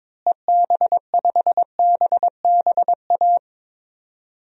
Morse code, heard as a keyed tone, about 22 wpm, 700 Hz. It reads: EB5BBA
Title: DXCC-100 Call Signs - 6 Characters - Repeated 3x - 22wpm